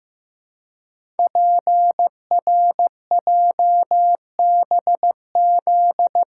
Morse code, heard as a keyed tone, about 15 wpm, 700 Hz. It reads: PRJBZ
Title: Code Groups - 5 Letters - 15wpm